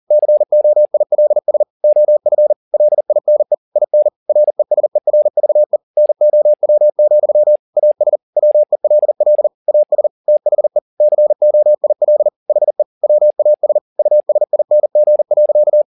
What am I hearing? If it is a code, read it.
Morse code, heard as a keyed tone, about 40 wpm, 600 Hz. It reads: COILS OF LINE IN RESERVE NOW, AS WELL AS THE COIL HE WAS USING.